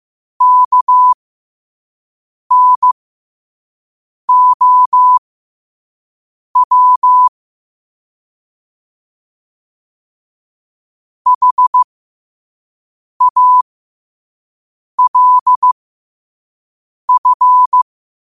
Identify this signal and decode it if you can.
Morse code, heard as a keyed tone, about 15 wpm, spaced as for 5 wpm, 1000 Hz. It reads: KNOW HALF
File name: qotdfiles / 5wpm